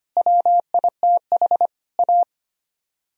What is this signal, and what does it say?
Morse code, heard as a keyed tone, about 29 wpm, 700 Hz. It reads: WITH A
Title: Speed-Racing - Sets of 4 Words 25wpm